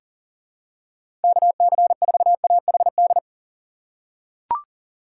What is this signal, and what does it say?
Morse code, heard as a keyed tone, about 37 wpm, 700 Hz. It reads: KC4AHD
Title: USA Call Signs - 6 Characters - 4x Character Spacing - 40wpm